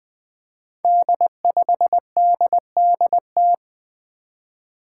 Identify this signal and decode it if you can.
Morse code, heard as a keyed tone, about 20 wpm, 700 Hz. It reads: D5DDT
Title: Five Challenging Letter-Number Combinations 20wpm